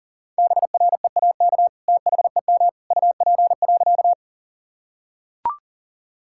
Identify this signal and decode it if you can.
Morse code, heard as a keyed tone, about 37 wpm, 700 Hz. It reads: BREAK THEM UP.
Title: Sentences from Top 1000 Words - Review Part 6 - 40wpm